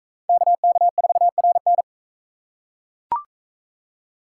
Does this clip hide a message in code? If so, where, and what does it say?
Yes: Morse code from 0.3 to 1.8 s, heard as a keyed tone, about 40 wpm, 700 Hz. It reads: KK4RN